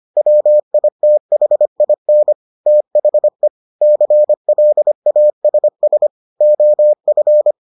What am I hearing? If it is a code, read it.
Morse code, heard as a keyed tone, about 25 wpm, 600 Hz. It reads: WITHIN THE CLASS OF